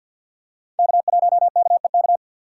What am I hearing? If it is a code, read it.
Morse code, heard as a keyed tone, about 46 wpm, 690 Hz. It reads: K1KEK
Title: USA Call Signs - 3x Character Spacing - 50wpm